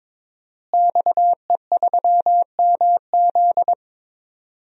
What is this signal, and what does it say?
Morse code, heard as a keyed tone, about 22 wpm, 700 Hz. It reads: XE3MZ
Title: DXCC-100 Call Signs - 5 Characters - Repeated 2x - 22wpm